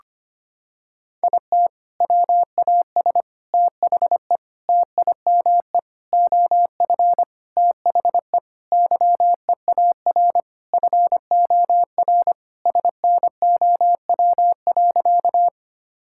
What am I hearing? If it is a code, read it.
Morse code, heard as a keyed tone, about 25 wpm, 700 Hz. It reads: IT WAS THE TIME OF THE YEAR FOR SNOW.
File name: practice / 25wpm / Sentences with 4-Word Phrases - 25wpm